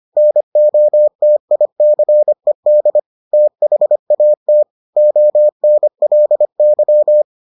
Morse code, heard as a keyed tone, about 25 wpm, 600 Hz. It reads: NOTICED THAT ONLY